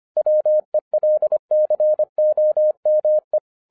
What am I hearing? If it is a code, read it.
Morse code, heard as a keyed tone, about 25 wpm, 610 Hz. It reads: WELCOME